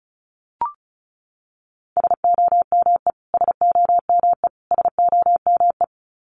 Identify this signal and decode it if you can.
Morse code, heard as a keyed tone, about 34 wpm, 700 Hz. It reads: SOME SOME SOME
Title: Top 200 Words Repeated 35wpm